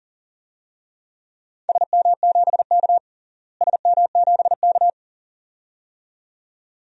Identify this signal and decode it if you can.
Morse code, heard as a keyed tone, about 40 wpm, 700 Hz. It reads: SM7K SM7K
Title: DXCC-100 Call Signs - 4 Characters - Repeated 2x - 40wpm